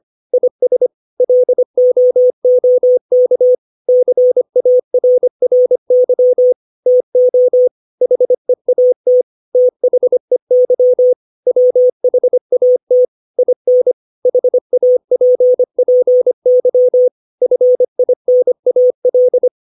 Morse code, heard as a keyed tone, about 25 wpm, 500 Hz. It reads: IS LOOK CARRY TO HEAT THEY WHAT IN HAPPY FINAL
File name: 632 words / 632 Words 25WPM